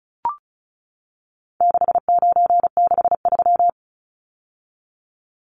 Morse code, heard as a keyed tone, about 34 wpm, 700 Hz. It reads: 6963
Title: Numbers - 4-Digit 35wpm